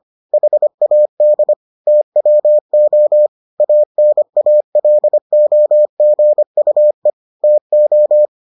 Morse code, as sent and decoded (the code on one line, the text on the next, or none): .... .- -.. / - .-- --- / .- -. .- .-.. --- --. ..- . / - ---
HAD TWO ANALOGUE TO